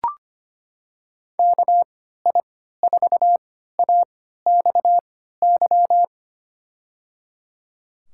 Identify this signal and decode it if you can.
Morse code, heard as a keyed tone, about 25 wpm, spaced as for 14 wpm, 700 Hz. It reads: KI4AXY
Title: USA Call Signs - 3x Character Spacing - 25wpm